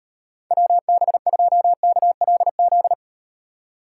Morse code, heard as a keyed tone, about 38 wpm, 700 Hz. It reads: WB2KLZ